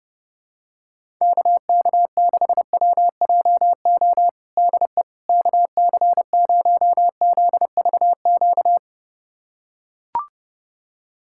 Morse code, heard as a keyed tone, about 30 wpm, 700 Hz. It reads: KK6WJO DE KC0ZVQ
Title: USA Callsign DE Callsign 30wpm